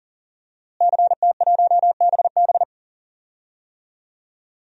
Morse code, heard as a keyed tone, about 40 wpm, 700 Hz. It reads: CT1BB
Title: DXCC-100 Call Signs - 5 Characters - Repeated 2x - 40wpm